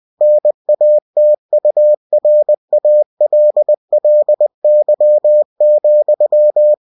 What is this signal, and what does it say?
Morse code, heard as a keyed tone, about 20 wpm, 600 Hz. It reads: NATURALLY,
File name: cw-books / winnie-the-pooh / winnie-20wpm-page-0101